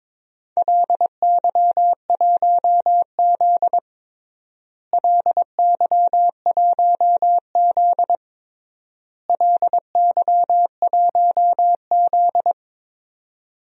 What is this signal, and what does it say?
Morse code, heard as a keyed tone, about 22 wpm, 700 Hz. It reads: LY1Z LY1Z LY1Z
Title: DXCC-100 Call Signs - 4 Characters - Repeated 3x - 22wpm